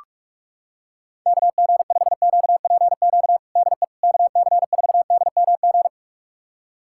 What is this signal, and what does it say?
Morse code, heard as a keyed tone, about 46 wpm, 700 Hz. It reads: KG5QPQ DE KC4DMG
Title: USA Callsign DE Callsign - 6 Characters - 45wpm